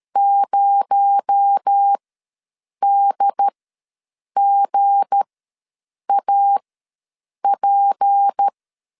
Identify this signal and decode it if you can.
Morse code, heard as a keyed tone, about 13 wpm, spaced as for 7 wpm, 800 Hz. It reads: 0DGAP
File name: LEZIONI / 7 GF9Z / 7 GF ws 12wpm 3140